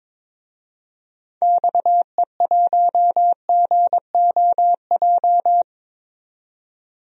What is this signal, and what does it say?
Morse code, heard as a keyed tone, about 22 wpm, 700 Hz. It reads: XE1GOJ
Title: DXCC-100 Call Signs - 6 Characters - 22wpm